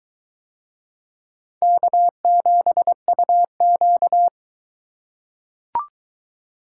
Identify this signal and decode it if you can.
Morse code, heard as a keyed tone, about 21 wpm, 700 Hz. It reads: K7UQ